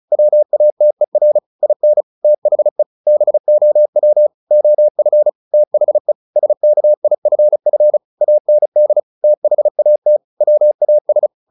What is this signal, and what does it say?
Morse code, heard as a keyed tone, about 35 wpm, 600 Hz. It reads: WATER IN THE BOW OF THE SKIFF AND THAT WAS
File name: the old man and the sea / the old man and the sea 1 35wpm